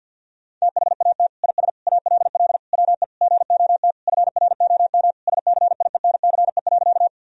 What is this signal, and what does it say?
Morse code, heard as a keyed tone, about 52 wpm, 700 Hz. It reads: THAT IS ALL WE GOT FROM SCIENCE.
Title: Sentences from Top 1000 Words - Review Part 2 - 50wpm